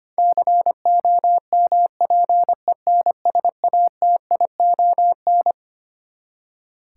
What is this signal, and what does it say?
Morse code, heard as a keyed tone, about 25 wpm, 700 Hz. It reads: COMPENSATION
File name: Twelve Letter Words - 25wpm